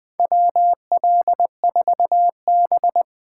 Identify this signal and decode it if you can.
Morse code, heard as a keyed tone, about 21 wpm, 700 Hz. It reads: WL4B
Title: USA Call Signs - 4 Characters - 4x Character Spacing - 20wpm